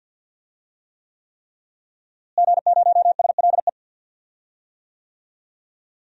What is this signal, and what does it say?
Morse code, heard as a keyed tone, about 46 wpm, 700 Hz. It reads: G0SLE
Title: DXCC-100 Call Signs - 5 Characters - Repeated 3x - 50wpm